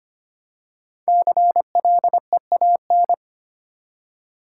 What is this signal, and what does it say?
Morse code, heard as a keyed tone, about 25 wpm, 700 Hz. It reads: CLEAN
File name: 5 Letter Words - First 3 Letters Given - 25wpm